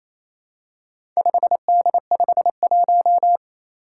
Morse code, heard as a keyed tone, about 29 wpm, 700 Hz. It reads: 5D51
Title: Four Challenging Letter-Number Combinations 28wpm